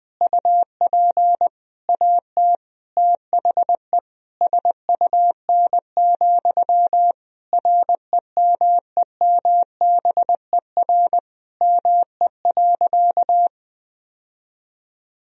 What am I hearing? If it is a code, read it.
Morse code, heard as a keyed tone, about 20 wpm, 700 Hz. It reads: UP AT THE SUN, REMEMBER ME.
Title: Sentences from Top 1000 Words - Review Part 1 - 20wpm